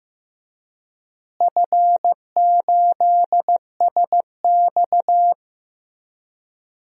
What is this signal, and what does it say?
Morse code, heard as a keyed tone, about 15 wpm, 700 Hz. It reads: F8SX
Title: DXCC-100 Call Signs - 4 Characters - Repeated 2x - 15wpm